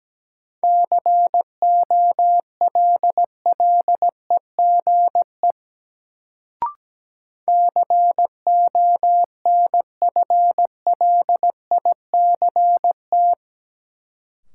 Morse code, heard as a keyed tone, about 17 wpm, 700 Hz. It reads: COLLEGE CONFLICT
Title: Sentences from Top 1300 Words - New Word Drill Alt - 17wpm